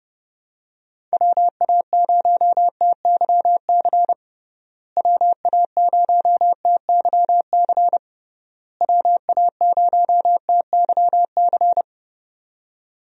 Morse code, heard as a keyed tone, about 30 wpm, 700 Hz. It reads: WA0TYC WA0TYC WA0TYC
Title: USA Call Signs - 6 Characters - Repeated 3x - 30wpm